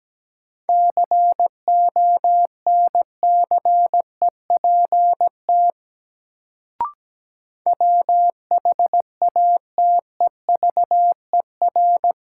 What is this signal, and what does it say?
Morse code, heard as a keyed tone, about 17 wpm, 700 Hz. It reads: CONCEPT WHATEVER
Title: Sentences from Top 1100 Words - New Word Drill - 17wpm